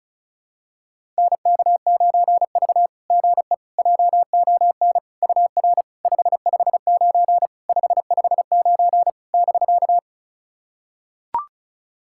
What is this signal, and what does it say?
Morse code, heard as a keyed tone, about 36 wpm, 690 Hz. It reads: NK9V GE JON UR 559 559 <BK>
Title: SOTA QSO Practice 35wpm